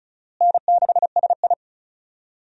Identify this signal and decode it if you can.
Morse code, heard as a keyed tone, about 34 wpm, 700 Hz. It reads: N6SI
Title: USA Call Signs - 4 Characters - Repeated 3x - 35wpm